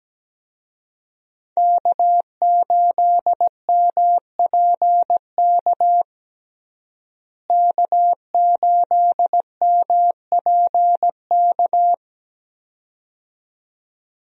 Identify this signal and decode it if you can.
Morse code, heard as a keyed tone, about 17 wpm, 700 Hz. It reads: K8MPK K8MPK
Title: USA Call Signs - 5 Characters - Repeated 2x - 17wpm